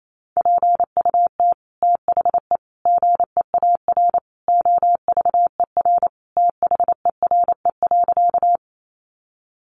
Morse code, heard as a keyed tone, about 28 wpm, 700 Hz. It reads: PUT THE GEAR OVER THERE.
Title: Sentences from Top 1400 Words 28wpm